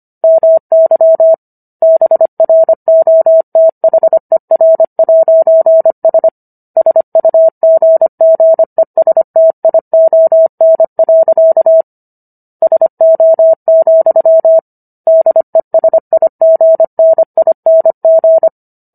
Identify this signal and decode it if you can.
Morse code, heard as a keyed tone, about 25 wpm, 650 Hz. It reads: MY BROTHER'S SUGGESTION. SO, DESIGNING